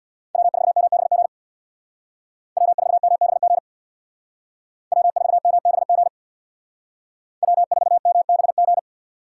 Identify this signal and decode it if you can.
Morse code, heard as a keyed tone, about 64 wpm, 700 Hz. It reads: W4MBG W4MBG W4MBG W4MBG